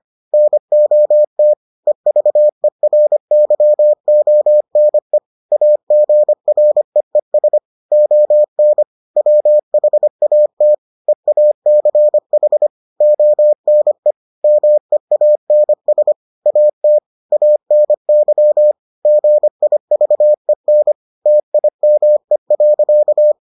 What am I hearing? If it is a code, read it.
Morse code, heard as a keyed tone, about 25 wpm, 600 Hz. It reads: NOT EVERYONE AGREES ON WHAT EACH ONE MEANS AT ANY GIVEN TIME.